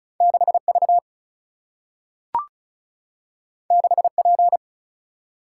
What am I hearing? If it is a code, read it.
Morse code, heard as a keyed tone, about 36 wpm, 700 Hz. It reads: 6V 6P
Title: Single-Letter Single-Number 35wpm